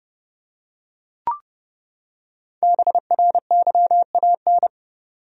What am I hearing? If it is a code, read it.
Morse code, heard as a keyed tone, about 29 wpm, 700 Hz. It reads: BRYAN
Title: USA Common Names - 30wpm